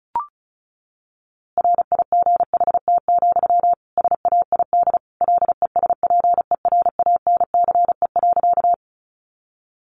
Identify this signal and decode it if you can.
Morse code, heard as a keyed tone, about 34 wpm, 700 Hz. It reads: RIGHT, SAID LESPERANCE.
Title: A Sound Of Thunder 35wpm